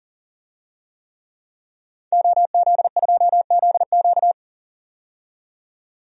Morse code, heard as a keyed tone, about 38 wpm, 700 Hz. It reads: OZ2ZQ
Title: DXCC-100 Call Signs - 5 Characters - 40wpm